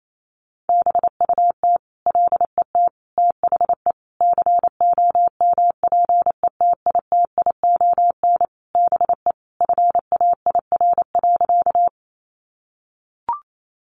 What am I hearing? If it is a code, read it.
Morse code, heard as a keyed tone, about 28 wpm, 700 Hz. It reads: BUT LET THE COMPETITION BE FAIR.
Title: Sentences from Top 1300 Words 28wpm